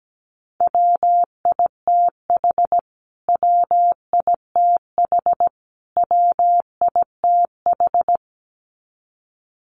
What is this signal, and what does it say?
Morse code, heard as a keyed tone, about 17 wpm, 700 Hz. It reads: WITH WITH WITH
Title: Top 500 Words Repeated 17wpm